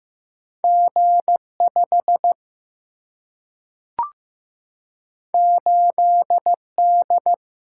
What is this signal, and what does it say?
Morse code, heard as a keyed tone, about 15 wpm, 700 Hz. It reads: G5 8D
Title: Single-Letter Single-Number 15wpm